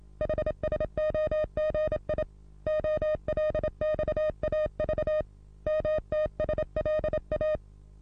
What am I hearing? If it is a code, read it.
Morse code, heard as a keyed tone, about 32 wpm, 620 Hz. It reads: HSOGI OLXAV MTSLA
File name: repos / Morsekurs / DJ4UF Morsen27